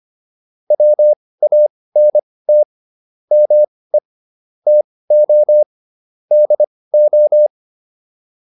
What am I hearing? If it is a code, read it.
Morse code, heard as a keyed tone, about 25 wpm, spaced as for 18 wpm, 600 Hz. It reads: WANT ME TO DO
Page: intermediate-practice-files